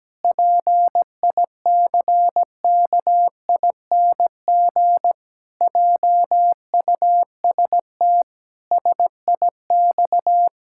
Morse code, heard as a keyed tone, about 17 wpm, 700 Hz. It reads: PICKING JUST SIX
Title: Pangram Challenge v2 - 17wpm